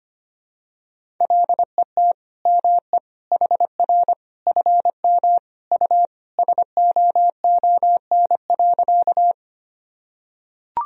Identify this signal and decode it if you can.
Morse code, heard as a keyed tone, about 25 wpm, 700 Hz. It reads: LET ME HR FM U SOON.
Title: Sentences from Top 400 Words and Abbreviations 25wpm